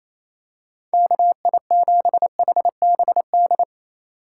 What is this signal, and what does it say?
Morse code, heard as a keyed tone, about 27 wpm, 700 Hz. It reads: KI7HBD
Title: USA Call Signs - 6 Characters - 4x Character Spacing - 28wpm